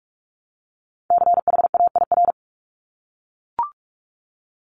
Morse code, heard as a keyed tone, about 46 wpm, 700 Hz. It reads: CHAIR
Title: Multi-letter Phonemes Repeated 3x 45wpm